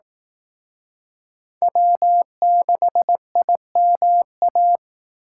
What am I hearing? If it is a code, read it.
Morse code, heard as a keyed tone, about 18 wpm, 700 Hz. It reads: W6IMA
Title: Speed Racing - USA 1x3 Call Signs 15wpm